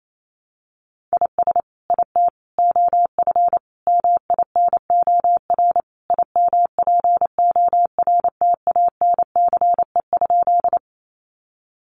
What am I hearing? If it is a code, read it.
Morse code, heard as a keyed tone, about 28 wpm, 700 Hz. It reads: IS IT OF MINOR IMPORTANCE?
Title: Sentences from Top 1500 Words 28wpm